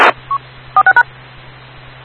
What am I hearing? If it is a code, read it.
DTMF, heard as telephone touch tones, tones of about 90 ms. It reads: 46#
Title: Rocky Mountain Radio League Repeaters